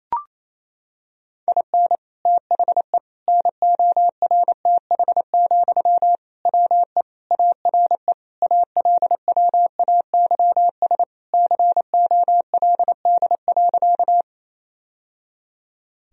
Morse code, heard as a keyed tone, about 28 wpm, 700 Hz. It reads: IN THE NORTH, WE ARE ALWAYS COLD.
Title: Sentences from Top 1000 Words - Review Part 1 - 28wpm